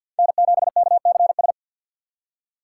Morse code, heard as a keyed tone, about 48 wpm, 700 Hz. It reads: N7KCS
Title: USA Call Signs - 4x Character Spacing - 50wpm